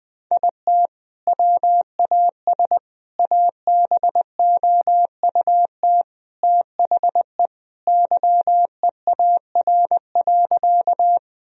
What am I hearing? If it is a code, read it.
Morse code, heard as a keyed tone, about 20 wpm, 700 Hz. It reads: IT WAS ABOUT THE YEAR.
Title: Sentences from Top 1000 Words - Review Part 1 - 20wpm